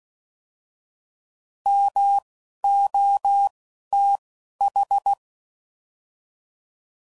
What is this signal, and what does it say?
Morse code, heard as a keyed tone, about 16 wpm, 800 Hz. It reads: M O T H